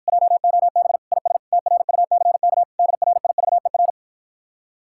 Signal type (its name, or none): Morse code